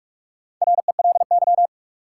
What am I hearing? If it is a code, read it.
Morse code, heard as a keyed tone, about 43 wpm, 700 Hz. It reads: RELY
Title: Sentences from Top 1200 Words - New Word Drill Alt - 45wpm